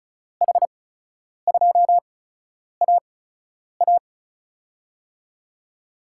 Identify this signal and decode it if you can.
Morse code, heard as a keyed tone, about 36 wpm, spaced as for 10 wpm, 700 Hz. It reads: H2AA
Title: USA Call Signs - 8x Character Spacing - 35wpm